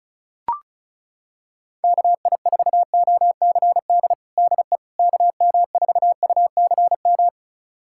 Morse code, heard as a keyed tone, about 34 wpm, 700 Hz. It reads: KI4OCD DE KM4UCM
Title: USA Callsign DE Callsign - 6 Characters - 35wpm